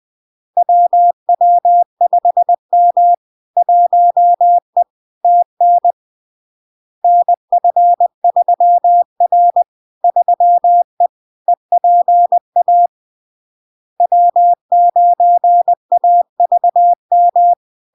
Morse code, heard as a keyed tone, about 20 wpm, 700 Hz. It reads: WW5M 1E TN NF3R 3E EPA W9AVM
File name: LARK-FD-2022-Log-in-CW-20wpm-700-Hz